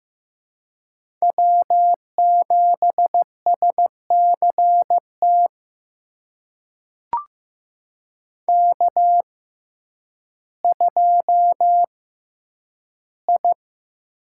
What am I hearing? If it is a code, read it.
Morse code, heard as a keyed tone, about 15 wpm, spaced as for 9 wpm, 700 Hz. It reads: W7SCT K2I
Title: USA Call Signs - 5 Characters - 6x Character Spacing - 15wpm